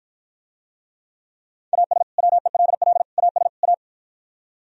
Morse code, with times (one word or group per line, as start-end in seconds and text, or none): 1.7-2.0 AS
2.2-3.0 WELL
3.2-3.5 AS
3.6-3.7 A